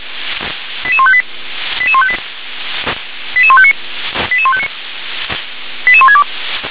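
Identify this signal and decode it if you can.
Five-tone selective call (ZVEI), heard as quick runs of five tones, tones of about 70 ms. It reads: 88158 88158 88159 88159 88162
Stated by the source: Received on an SDR, ZVEI five-tone code 88020.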